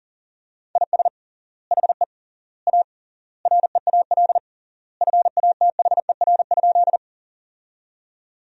Morse code, heard as a keyed tone, about 37 wpm, 700 Hz. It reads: IS HE A REAL FATHER?
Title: 3x Word-Spacing - Sentences from Top 300 Words - 40wpm